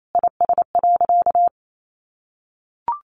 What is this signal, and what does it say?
Morse code, heard as a keyed tone, about 28 wpm, 700 Hz. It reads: IS.
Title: Sentences from Top 1100 Words 28wpm